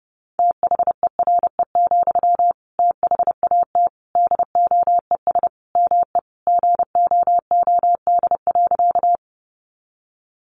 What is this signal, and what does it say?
Morse code, heard as a keyed tone, about 30 wpm, 700 Hz. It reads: THERE, THAT DOES ME GOOD.
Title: Sentences from Top 200 Words 30wpm